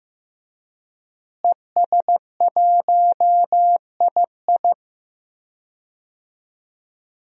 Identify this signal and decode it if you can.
Morse code, heard as a keyed tone, about 15 wpm, 700 Hz. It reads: ES1II
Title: DXCC-100 Call Signs - 5 Characters - Repeated 3x - 15wpm